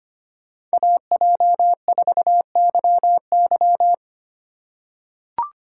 Morse code, heard as a keyed tone, about 25 wpm, 700 Hz. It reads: AJ4YY
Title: USA Call Signs - 5 Characters - 3x Character Spacing - 25wpm